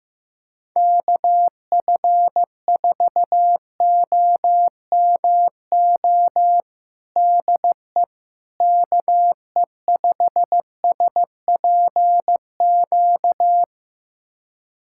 Morse code, heard as a keyed tone, about 15 wpm, 700 Hz. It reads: KF4OMO DE KE5SPQ
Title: USA Callsign DE Callsign - 6 Characters - 15wpm